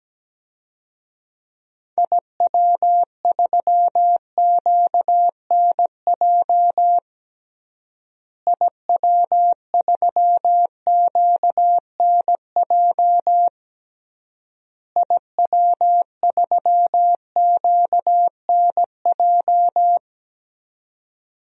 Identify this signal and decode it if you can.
Morse code, heard as a keyed tone, about 17 wpm, 700 Hz. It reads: IW3QNJ IW3QNJ IW3QNJ
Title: DXCC-100 Call Signs - 6 Characters - Repeated 3x - 17wpm